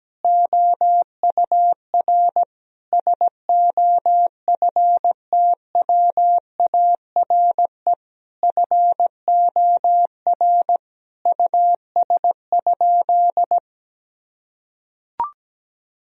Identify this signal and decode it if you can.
Morse code, heard as a keyed tone, about 17 wpm, 700 Hz. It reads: OUR SOFTWARE FOR US?
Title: Sentences from Top 1000 Words - Review Part 3 - 17wpm